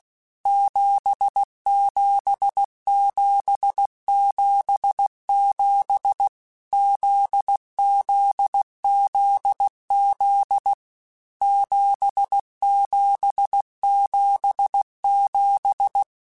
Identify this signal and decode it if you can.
Morse code, heard as a keyed tone, about 16 wpm, 800 Hz. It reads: 77777 ZZZZ 7777
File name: Basic Letter Groups By7Groups / JZ17 / JZ17@15